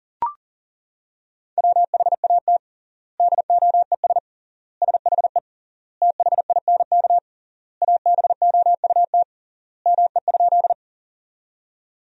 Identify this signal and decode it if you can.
Morse code, heard as a keyed tone, about 40 wpm, 700 Hz. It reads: WHAT DOES SHE THINK ABOUT ME?
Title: 3x Word-Spacing - Sentences from Top 200 Words - 40wpm